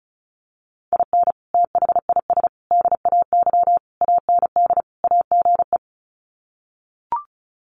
Morse code, heard as a keyed tone, about 34 wpm, 700 Hz. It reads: IN THIS DAY AND AGE
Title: Sets of 5 Words Encore 35wpm